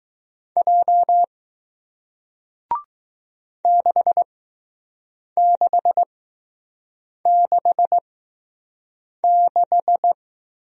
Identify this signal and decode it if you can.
Morse code, heard as a keyed tone, about 20 wpm, spaced as for 6 wpm, 700 Hz. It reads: J 6666